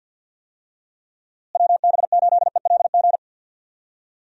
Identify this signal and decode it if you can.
Morse code, heard as a keyed tone, about 52 wpm, 700 Hz. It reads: WB8ELG
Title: USA Call Signs - 6 Characters - Repeated 2x - 50wpm